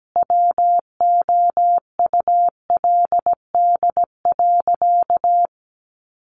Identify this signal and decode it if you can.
Morse code, heard as a keyed tone, about 17 wpm, 700 Hz. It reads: WOULD.
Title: Sentences from Top 1500 Words 17wpm